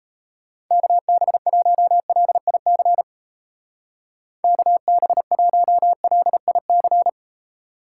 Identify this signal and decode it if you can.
Morse code, heard as a keyed tone, about 34 wpm, 700 Hz. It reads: KB1LIC KB1LIC
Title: Speed Racing - USA 2x3 Call Signs 28wpm